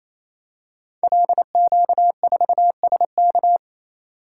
Morse code, heard as a keyed tone, about 27 wpm, 700 Hz. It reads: LQ4SK
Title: Code Groups - 5 Letter:Numbers - 28wpm